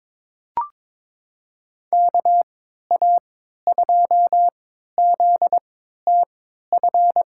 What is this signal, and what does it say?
Morse code, heard as a keyed tone, about 22 wpm, spaced as for 12 wpm, 700 Hz. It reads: KA2ZTF
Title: USA Call Signs - 6 Characters - 3x Character Spacing - 22wpm